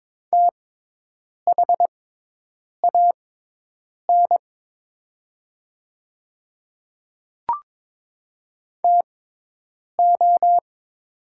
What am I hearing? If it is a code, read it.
Morse code, heard as a keyed tone, about 22 wpm, spaced as for 8 wpm, 700 Hz. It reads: THAN TO
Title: 6x Character Spacing - Top 100 Words - 22wpm